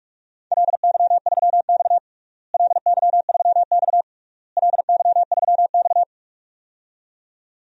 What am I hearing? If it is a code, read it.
Morse code, heard as a keyed tone, about 42 wpm, 700 Hz. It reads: LY3X LY3X LY3X
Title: DXCC-100 Call Signs - 4 Characters - Repeated 3x - 45wpm